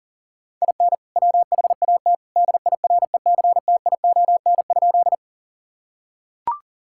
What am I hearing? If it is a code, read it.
Morse code, heard as a keyed tone, about 37 wpm, 700 Hz. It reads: IN WHAT DIRECTION?
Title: Sentences from Top 1000 Words - Review Part 5 - 40wpm